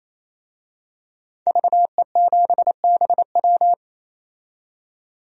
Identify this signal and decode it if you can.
Morse code, heard as a keyed tone, about 28 wpm, 700 Hz. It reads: VE7BW
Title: DXCC-100 Call Signs - 5 Characters - 28wpm